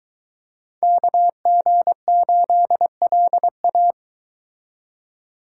Morse code, heard as a keyed tone, about 22 wpm, 700 Hz. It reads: KG8LA